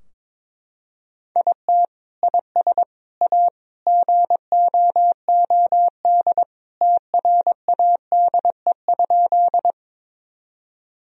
Morse code, heard as a keyed tone, about 22 wpm, 700 Hz. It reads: IT IS A GOOD TRADE?
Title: Sentences from Top 1000 Words - Review Part 4 - 22wpm